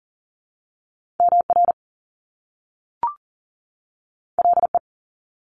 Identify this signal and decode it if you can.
Morse code, heard as a keyed tone, about 40 wpm, 700 Hz. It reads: MR LE
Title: Top Two Letter Combinations in Words 40wpm